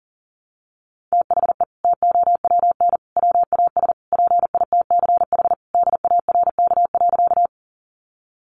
Morse code, heard as a keyed tone, about 40 wpm, 700 Hz. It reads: THE TOWN WAS PITCH DARK.
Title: Sentences from Top 1300 Words 40wpm